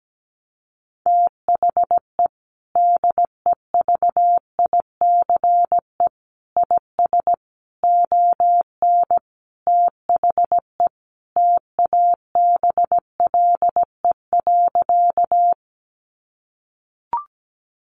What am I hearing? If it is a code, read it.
Morse code, heard as a keyed tone, about 17 wpm, 700 Hz. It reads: THE DEVICE IS ON THE TABLE.
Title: Sentences from Top 1500 Words 17wpm